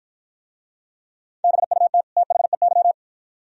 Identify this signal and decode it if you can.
Morse code, heard as a keyed tone, about 50 wpm, 700 Hz. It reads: BUT THEY